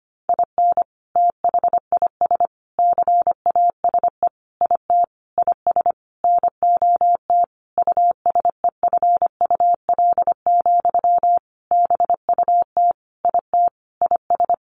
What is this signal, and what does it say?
Morse code, heard as a keyed tone, about 25 wpm, 700 Hz. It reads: IN THIS CASE IT IS NOT USEFUL, BUT IT IS
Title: Sentences from Top 1400 Words 25wpm